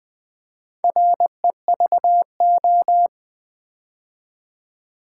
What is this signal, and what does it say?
Morse code, heard as a keyed tone, about 20 wpm, 700 Hz. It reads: REVO
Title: Build A Word - 4 Letters At A Time - 10-Letter Words - 20wpm